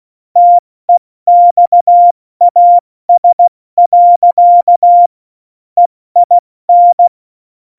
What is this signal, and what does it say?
Morse code, heard as a keyed tone, about 16 wpm, spaced as for 14 wpm, 700 Hz. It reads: TEXAS. EIN
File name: Morsekurs / Witze / Witz 01. Dezember 2020, 70 F80